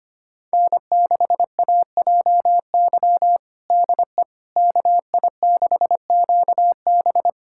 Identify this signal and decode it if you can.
Morse code, heard as a keyed tone, about 25 wpm, 700 Hz. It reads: N6AJY DE KI6QB
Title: USA Callsign DE Callsign - 5 Characters - 25wpm